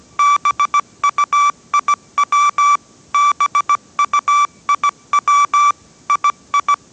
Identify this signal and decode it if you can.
Morse code, heard as a keyed tone, about 17 wpm, 1160 Hz. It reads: BUIW BUIW II
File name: 18wpm